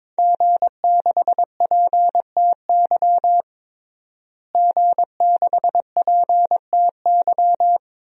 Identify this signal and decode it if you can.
Morse code, heard as a keyed tone, about 22 wpm, 700 Hz. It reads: G6PTY G6PTY
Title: DXCC-100 Call Signs - 5 Characters - Repeated 2x - 22wpm